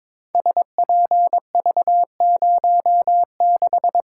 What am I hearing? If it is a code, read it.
Morse code, heard as a keyed tone, about 22 wpm, 700 Hz. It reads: SPV06
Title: Code Groups - 5 Letter:Numbers - 22wpm